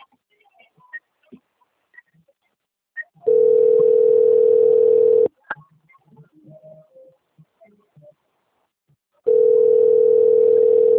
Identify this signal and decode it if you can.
Ringback tone (North America), 440 + 480 Hz.